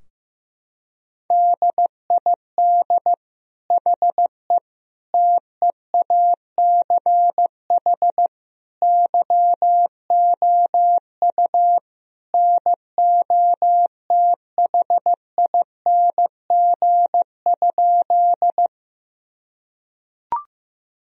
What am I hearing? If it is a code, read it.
Morse code, heard as a keyed tone, about 15 wpm, 700 Hz. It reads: DID HE TEACH YOU NOTHING?
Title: Sentences from Top 1000 Words - Review Part 2 - 15wpm